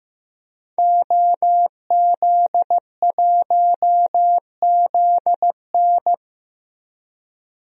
Morse code, heard as a keyed tone, about 15 wpm, 700 Hz. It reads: OZ1ZN
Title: DXCC-100 Call Signs - 5 Characters - 15wpm